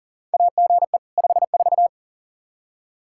Morse code, heard as a keyed tone, about 39 wpm, 700 Hz. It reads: AGE 54
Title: Practice QSO - Exchange 3 - More Info - 40wpm